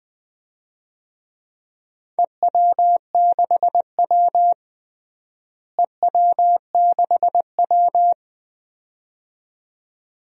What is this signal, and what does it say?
Morse code, heard as a keyed tone, about 21 wpm, 700 Hz. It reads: EW6W EW6W
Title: DXCC-100 Call Signs - 4 Characters - Repeated 2x - 20wpm